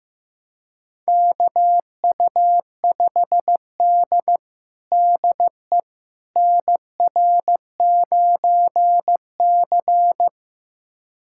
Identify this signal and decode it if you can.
Morse code, heard as a keyed tone, about 15 wpm, 700 Hz. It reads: KU5D DE NR9C
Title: USA Callsign DE Callsign - 4 Characters - 15wpm